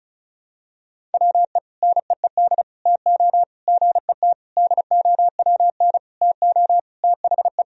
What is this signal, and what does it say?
Morse code, heard as a keyed tone, about 34 wpm, 700 Hz. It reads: WE NEED TO GET DOWN TO THE
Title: Sentences from Top 1000 Words - Review Part 1 - 35wpm